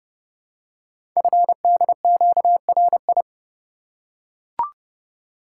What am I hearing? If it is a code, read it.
Morse code, heard as a keyed tone, about 30 wpm, 700 Hz. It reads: FDQRI
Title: Code Groups - 5 Letters - 30wpm